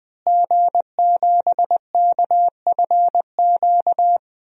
Morse code, heard as a keyed tone, about 20 wpm, 700 Hz. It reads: G7KFQ
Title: DXCC-100 Call Signs - 5 Characters - Repeated 2x - 20wpm